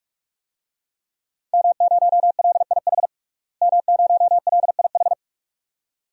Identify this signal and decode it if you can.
Morse code, heard as a keyed tone, about 46 wpm, 700 Hz. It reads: M0LIH M0LIH
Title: DXCC-100 Call Signs - 5 Characters - Repeated 2x - 45wpm